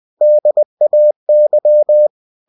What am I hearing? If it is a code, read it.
Morse code, heard as a keyed tone, about 20 wpm, 600 Hz. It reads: DAY